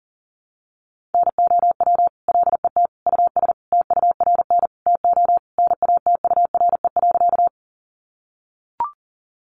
Morse code, heard as a keyed tone, about 40 wpm, 700 Hz. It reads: NOW LET US TURN TO NATURE.
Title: Sentences from Top 600 Words 40wpm